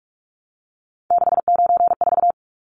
Morse code, heard as a keyed tone, about 44 wpm, 700 Hz. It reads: 694
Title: Numbers - 3-Digit 45wpm